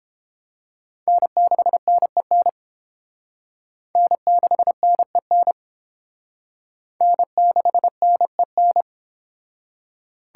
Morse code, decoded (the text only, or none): N6NEN N6NEN N6NEN